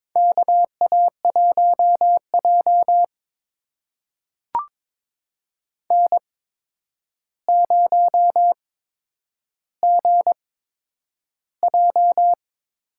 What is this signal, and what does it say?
Morse code, heard as a keyed tone, about 22 wpm, spaced as for 9 wpm, 700 Hz. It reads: KA1J N0GJ
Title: USA Call Signs - 8x Character Spacing - 22wpm